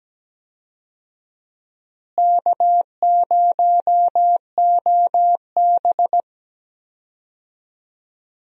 Morse code, heard as a keyed tone, about 17 wpm, 700 Hz. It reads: K0OB
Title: USA Call Signs - 4 Characters - Repeated 3x - 17wpm